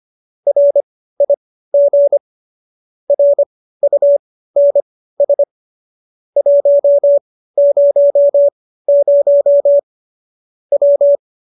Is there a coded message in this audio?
Yes: Morse code from 0.5 to 11.2 s, heard as a keyed tone, about 25 wpm, spaced as for 15 wpm, 570 Hz. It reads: RIG RUNS 100 W